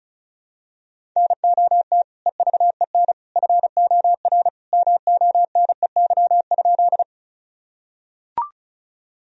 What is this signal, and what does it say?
Morse code, heard as a keyed tone, about 34 wpm, 700 Hz. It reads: NOT EVEN FOR MONEY?
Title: Sentences from Top 1000 Words - Review Part 1 - 35wpm